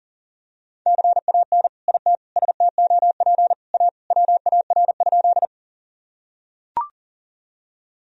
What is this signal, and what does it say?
Morse code, heard as a keyed tone, about 40 wpm, 700 Hz. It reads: CAN IT STOP A WAR?
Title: Sentences from Top 1000 Words - Review Part 2 - 40wpm